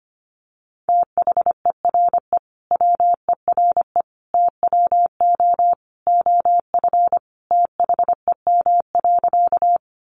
Morse code, heard as a keyed tone, about 25 wpm, 700 Hz. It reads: THERE WERE TWO OF THEM.
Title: Sentences from Top 100 Words 25wpm